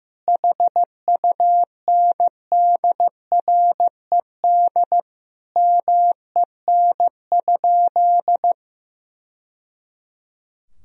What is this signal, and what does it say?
Morse code, heard as a keyed tone, about 15 wpm, 700 Hz. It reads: HUNDRED MEN?
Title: Sentences from Top 1000 Words - Review Part 2 - 15wpm